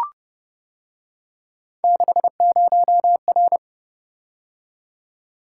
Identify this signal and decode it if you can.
Morse code, heard as a keyed tone, about 29 wpm, 700 Hz. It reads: 60R